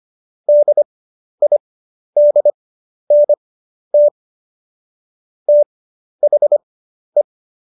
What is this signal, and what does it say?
Morse code, heard as a keyed tone, about 25 wpm, spaced as for 11 wpm, 590 Hz. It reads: DIDNT THE
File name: Jokes-25-char-10-eff